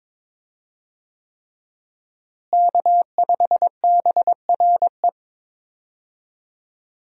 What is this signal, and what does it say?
Morse code, heard as a keyed tone, about 22 wpm, 700 Hz. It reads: K5BRE
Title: USA Call Signs - 5 Characters - Repeated 3x - 22wpm